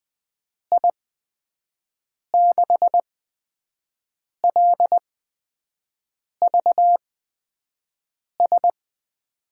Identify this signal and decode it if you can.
Morse code, heard as a keyed tone, about 20 wpm, spaced as for 5 wpm, 700 Hz. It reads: I6LVS
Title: USA Call Signs - 8x Character Spacing - 20wpm